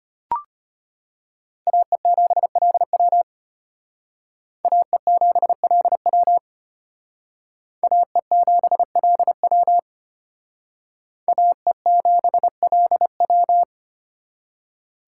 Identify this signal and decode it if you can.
Morse code, heard as a keyed tone, about 32 wpm, 700 Hz. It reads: AE7LW AE7LW AE7LW AE7LW